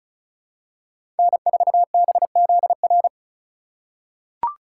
Morse code, heard as a keyed tone, about 36 wpm, 700 Hz. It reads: N4BZR
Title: USA Call Signs - 5 Characters - 4x Character Spacing - 35wpm